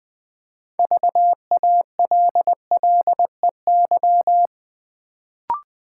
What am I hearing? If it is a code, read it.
Morse code, heard as a keyed tone, about 20 wpm, 700 Hz. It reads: VALLEY
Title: Build A Word - 2 Letters At A Time - 6-Letter Words - 20wpm